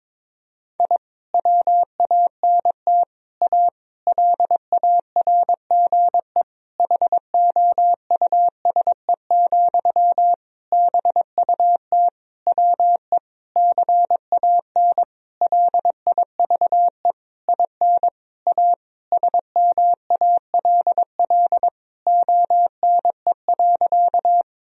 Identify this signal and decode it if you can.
Morse code, heard as a keyed tone, about 22 wpm, 700 Hz. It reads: I WANT A LARGE HOUSE, BUT WE CAN LIVE IN A SMALL ONE.
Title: Sentences from Top 1000 Words - Review Part 1 - 22wpm